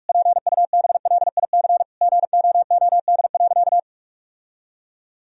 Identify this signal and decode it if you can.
Morse code, heard as a keyed tone, about 46 wpm, 700 Hz. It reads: PUBLIC GOOD.
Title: Sentences from Top 1000 Words - Review Part 3 - 45wpm